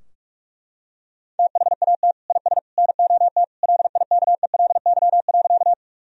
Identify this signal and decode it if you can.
Morse code, heard as a keyed tone, about 41 wpm, 700 Hz. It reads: THAT IS NOT LIKELY.
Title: Sentences from Top 1000 Words - Review Part 4 - 45wpm